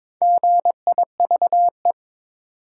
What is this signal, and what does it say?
Morse code, heard as a keyed tone, about 23 wpm, 700 Hz. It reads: GIVE
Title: Sentences from Top 200 Words - New Word Drill - 22wpm